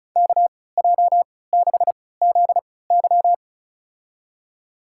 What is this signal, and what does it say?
Morse code, heard as a keyed tone, about 36 wpm, spaced as for 20 wpm, 700 Hz. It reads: KJ6ZY
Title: USA Call Signs - 3x Character Spacing - 35wpm